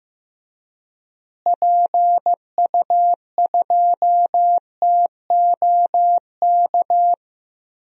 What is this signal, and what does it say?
Morse code, heard as a keyed tone, about 15 wpm, 700 Hz. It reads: PU2TOK
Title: DXCC-100 Call Signs - 6 Characters - Repeated 3x - 15wpm